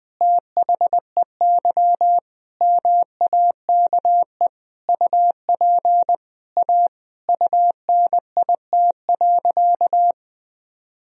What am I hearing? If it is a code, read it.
Morse code, heard as a keyed tone, about 20 wpm, 700 Hz. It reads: THEY MAKE UP A UNIT.
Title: Sentences from Top 1000 Words - Review Part 2 - 20wpm